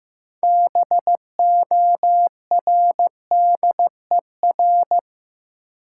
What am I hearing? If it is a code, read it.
Morse code, heard as a keyed tone, about 15 wpm, 700 Hz. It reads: BORDER
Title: Six Letter Words - 15wpm